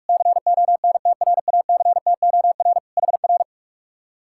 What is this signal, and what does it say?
Morse code, heard as a keyed tone, about 59 wpm, 700 Hz. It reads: CONTRACTOR HR